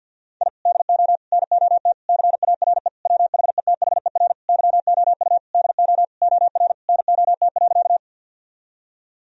Morse code, heard as a keyed tone, about 50 wpm, 700 Hz. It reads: I DO NOT CARE WHETHER YOU DO OR NOT.
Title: Sentences from Top 1000 Words - Review Part 3 - 50wpm